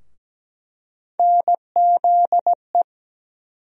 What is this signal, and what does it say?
Morse code, heard as a keyed tone, about 17 wpm, 700 Hz. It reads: NZE
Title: USA Call Sign Postfixes - 3 Letters 17wpm